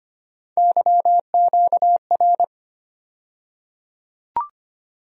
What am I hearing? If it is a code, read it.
Morse code, heard as a keyed tone, about 25 wpm, 700 Hz. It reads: YQR
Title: USA Call Sign Postfixes - 3 Letters 25wpm